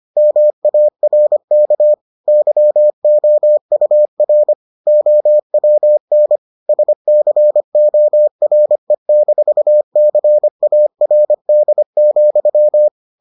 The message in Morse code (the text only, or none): MARK YOUR OWN SCORE-CARD,